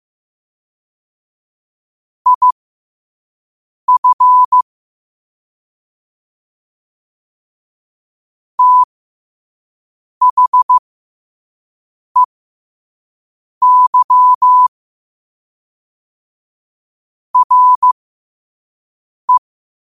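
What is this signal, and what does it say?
Morse code, heard as a keyed tone, about 15 wpm, spaced as for 5 wpm, 1000 Hz. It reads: IF THEY RE
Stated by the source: Morse code at 5 wpm.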